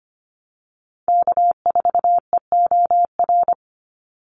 Morse code, heard as a keyed tone, about 25 wpm, 700 Hz. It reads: K4EOR
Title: Call Signs - 25wpm